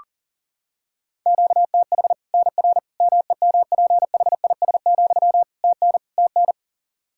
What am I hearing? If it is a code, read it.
Morse code, heard as a keyed tone, about 40 wpm, 700 Hz. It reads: QTH NR MEMPHIS, TN TN
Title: Practice QSO - Exchange 1 - Beginning - 40wpm